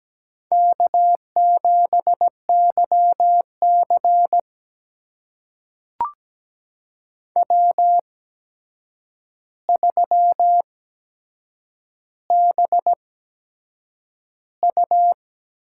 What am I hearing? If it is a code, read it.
Morse code, heard as a keyed tone, about 17 wpm, spaced as for 7 wpm, 700 Hz. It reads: K7YC W3BU
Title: USA Call Signs - 4 Characters - 8x Character Spacing - 17wpm